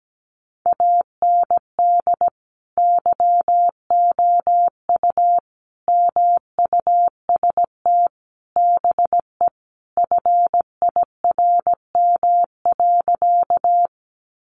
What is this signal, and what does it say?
Morse code, heard as a keyed tone, about 17 wpm, 700 Hz. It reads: AND YOU MUST BE FIRM.
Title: Sentences from Top 1400 Words 17wpm